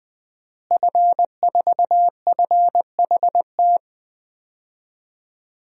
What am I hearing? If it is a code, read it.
Morse code, heard as a keyed tone, about 20 wpm, 700 Hz. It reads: F4FHT
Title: DXCC-100 Call Signs - 5 Characters - Repeated 2x - 20wpm